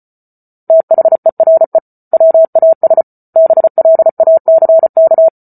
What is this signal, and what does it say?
Morse code, heard as a keyed tone, about 34 wpm, 650 Hz. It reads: THERE WAS BLACK